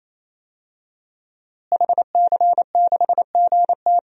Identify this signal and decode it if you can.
Morse code, heard as a keyed tone, about 28 wpm, 700 Hz. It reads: HC6GT
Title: DXCC-100 Call Signs - 5 Characters - Repeated 3x - 28wpm